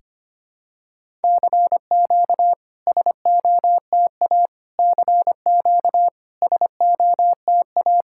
Morse code, heard as a keyed tone, about 25 wpm, 690 Hz. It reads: CQ SOTA CQ SOTA
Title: SOTA QSO Practice 25wpm